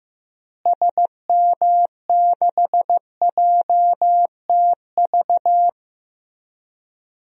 Morse code, heard as a keyed tone, about 15 wpm, 700 Hz. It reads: SM6JTV
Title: DXCC-100 Call Signs - 6 Characters - Repeated 2x - 15wpm